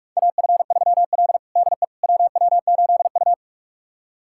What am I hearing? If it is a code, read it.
Morse code, heard as a keyed tone, about 46 wpm, 700 Hz. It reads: AF3L DE WW8U
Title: USA Callsign DE Callsign - 4 Characters - 45wpm